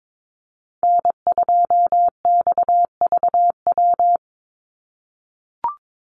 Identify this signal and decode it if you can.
Morse code, heard as a keyed tone, about 22 wpm, 700 Hz. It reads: N2XVW
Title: USA Call Signs 22wpm